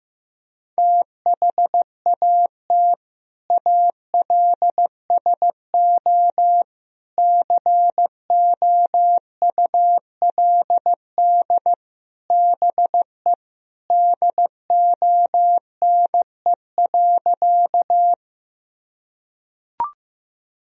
Morse code, heard as a keyed tone, about 15 wpm, 700 Hz. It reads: THAT ALSO COULD BE DONE.
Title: Sentences from Top 1000 Words - Review Part 1 - 15wpm